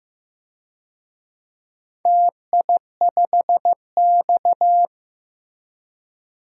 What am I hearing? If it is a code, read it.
Morse code, heard as a keyed tone, about 15 wpm, 700 Hz. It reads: TI5X
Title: DXCC-100 Call Signs - 4 Characters - Repeated 2x - 15wpm